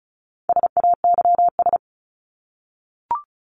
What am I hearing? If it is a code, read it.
Morse code, heard as a keyed tone, about 34 wpm, 700 Hz. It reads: SAYS
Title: ES:S Suffix - Top 100 Words with Base Words 35wpm